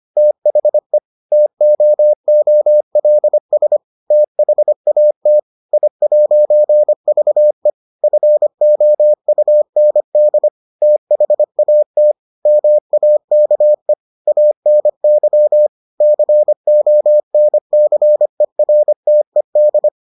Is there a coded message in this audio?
Yes: Morse code from 0.2 to 19.9 s, heard as a keyed tone, about 25 wpm, 600 Hz. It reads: THE TOOLS THAT I'VE FOUND THAT MAKE ANY CONCERTED